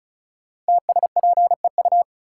Morse code, heard as a keyed tone, about 36 wpm, 700 Hz. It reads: TSPEU
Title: Code Groups - 5 Letter:Numbers - 35wpm